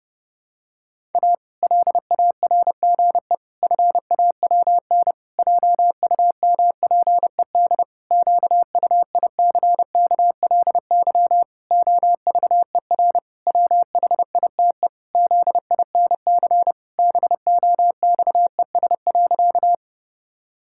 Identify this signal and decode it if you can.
Morse code, heard as a keyed tone, about 30 wpm, 700 Hz. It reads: A LARGE FAWN JUMPED QUICKLY OVER WHITE ZINC BOXES.
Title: Pangram Challenge v2 - 30wpm